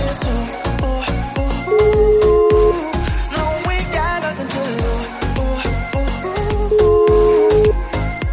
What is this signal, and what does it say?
Ringback tone (Europe, 425 Hz), with noise.